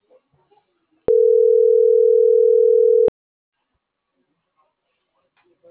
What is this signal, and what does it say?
Ringback tone (North America), 440 + 480 Hz.